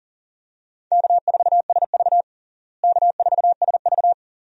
Morse code, heard as a keyed tone, about 37 wpm, 700 Hz. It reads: K4SV K4SV
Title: USA Call Signs - 4 Characters - Repeated 2x - 40wpm